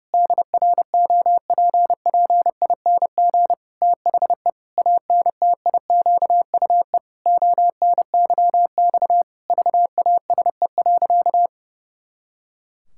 Morse code, heard as a keyed tone, about 30 wpm, 700 Hz. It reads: DROPPING THE ANTIQUE ONYX VASE.
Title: Pangram Challenge v2 - 30wpm